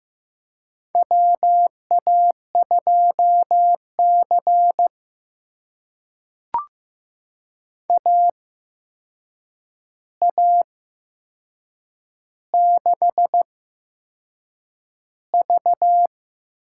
Morse code, heard as a keyed tone, about 15 wpm, spaced as for 6 wpm, 700 Hz. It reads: WA2C AA6V
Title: USA Call Signs - 4 Characters - 8x Character Spacing - 15wpm